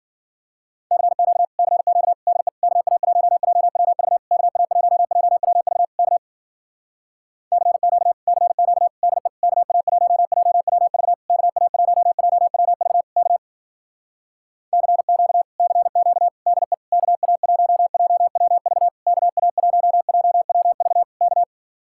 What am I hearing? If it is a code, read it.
Morse code, heard as a keyed tone, about 52 wpm, 700 Hz. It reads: CQ CQ DE KA1JWV KA1JWV K CQ CQ DE KA1JWV KA1JWV K CQ CQ DE KA1JWV KA1JWV K